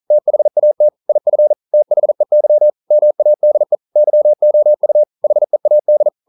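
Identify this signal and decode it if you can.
Morse code, heard as a keyed tone, about 42 wpm, 600 Hz. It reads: THAT IF THEY MADE YOU HEAD
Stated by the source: Morse code practice at 41 wpm, from CW med Gnister.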